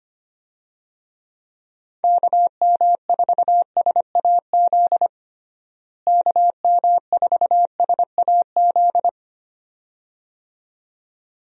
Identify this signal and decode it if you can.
Morse code, heard as a keyed tone, about 25 wpm, 700 Hz. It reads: KM4SAZ KM4SAZ
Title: USA Call Signs - 6 Characters - Repeated 2x - 25wpm